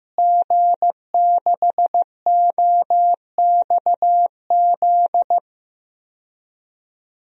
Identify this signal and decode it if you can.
Morse code, heard as a keyed tone, about 15 wpm, 700 Hz. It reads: G6OXZ
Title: DXCC-100 Call Signs - 5 Characters - Repeated 3x - 15wpm